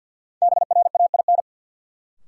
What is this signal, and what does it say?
Morse code, heard as a keyed tone, about 53 wpm, 700 Hz. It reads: BRAIN
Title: Sentences from Top 1100 Words - New Word Drill Alt - 50wpm